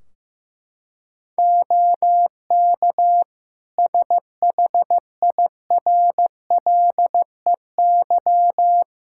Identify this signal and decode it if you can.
Morse code, heard as a keyed tone, about 15 wpm, 700 Hz. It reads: OK SHIRLEY
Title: Practice QSO - Exchange 3 - More Info - 15wpm